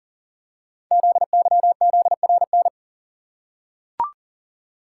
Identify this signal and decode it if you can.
Morse code, heard as a keyed tone, about 39 wpm, 690 Hz. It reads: ZYZRN
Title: Code Groups - 5 Letters - 40wpm